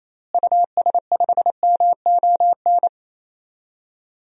Morse code, heard as a keyed tone, about 28 wpm, 700 Hz. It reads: US5MON
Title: DXCC-100 Call Signs - 6 Characters - Repeated 3x - 28wpm